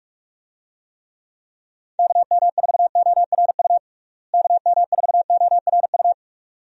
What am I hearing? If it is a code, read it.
Morse code, heard as a keyed tone, about 48 wpm, 700 Hz. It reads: KM4ORU KM4ORU
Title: USA Call Signs - 6 Characters - Repeated 2x - 45wpm